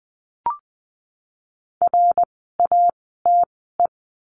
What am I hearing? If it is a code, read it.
Morse code, heard as a keyed tone, about 20 wpm, spaced as for 14 wpm, 700 Hz. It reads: RATE
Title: 054 Lesson 10f - Letter r - Words 20wpm